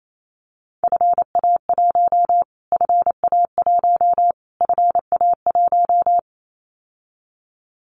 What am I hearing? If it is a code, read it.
Morse code, heard as a keyed tone, about 28 wpm, 700 Hz. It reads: FA1 FA1 FA1
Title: Callsign Prefixes for Top 25 DXCC - Repeated 3x - 28wpm